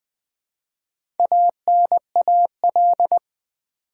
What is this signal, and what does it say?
Morse code, heard as a keyed tone, about 20 wpm, 700 Hz. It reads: ANAL
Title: Build A Word - 4 Letters At A Time - 8-Letter Words - 20wpm